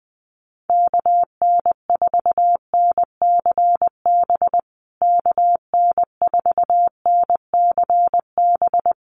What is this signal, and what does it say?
Morse code, heard as a keyed tone, about 20 wpm, 700 Hz. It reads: KN4NCB KN4NCB
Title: USA Call Signs Repeated 3x 20wpm